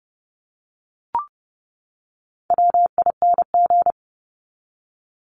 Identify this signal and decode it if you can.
Morse code, heard as a keyed tone, about 30 wpm, 700 Hz. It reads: WING
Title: Top 1001-1250 Words 30wpm